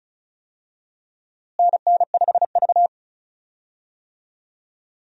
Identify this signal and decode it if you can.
Morse code, heard as a keyed tone, about 36 wpm, 700 Hz. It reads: NN5V
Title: USA Call Signs - 4 Characters - Repeated 3x - 35wpm